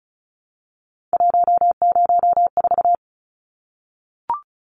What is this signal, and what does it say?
Morse code, heard as a keyed tone, about 34 wpm, 700 Hz. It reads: 104
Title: Numbers - 3-Digit 35wpm